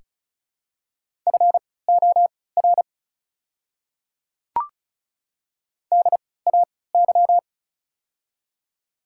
Morse code, heard as a keyed tone, about 36 wpm, spaced as for 20 wpm, 700 Hz. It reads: FOR DAY
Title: 3x Character Spacing - Top 200 Words - 35wpm